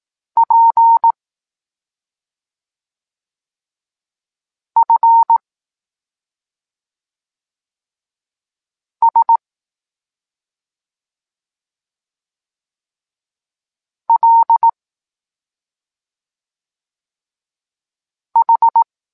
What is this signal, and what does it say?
Morse code, heard as a keyed tone, about 18 wpm, 920 Hz. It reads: P F S L H
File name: Tekst-26 1